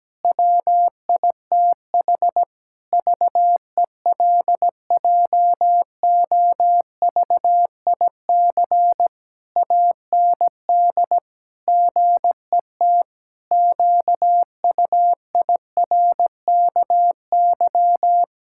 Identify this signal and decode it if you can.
Morse code, heard as a keyed tone, about 17 wpm, 700 Hz. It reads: WITH VELJOVIC AND GET QUIRKY
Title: Pangram Challenge v2 - 17wpm